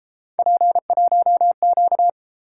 Morse code, heard as a keyed tone, about 33 wpm, 700 Hz. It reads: P1Q